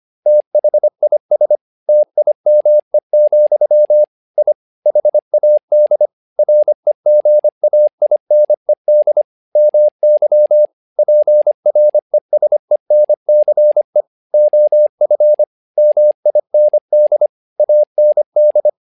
Morse code, heard as a keyed tone, about 25 wpm, 600 Hz. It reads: THIS TIME, I HAD REGAINED MY PRESENCE OF MIND AND
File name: learning center / tower trouble / tower trouble 25wpm